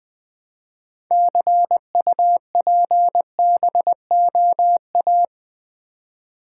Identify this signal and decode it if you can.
Morse code, heard as a keyed tone, about 20 wpm, 700 Hz. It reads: CUPBOA
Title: Build A Word - 3 Letters At A Time - 8-Letter Words - 20wpm